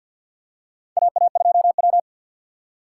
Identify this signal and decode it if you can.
Morse code, heard as a keyed tone, about 51 wpm, 700 Hz. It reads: AA2W
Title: USA Call Signs - 4x Character Spacing - 50wpm